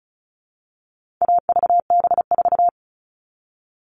Morse code, heard as a keyed tone, about 34 wpm, 700 Hz. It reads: AVB4
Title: All Cut Numbers - 4-Digit 35wpm